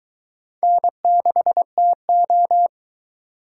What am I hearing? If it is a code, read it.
Morse code, heard as a keyed tone, about 21 wpm, 700 Hz. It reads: N6TO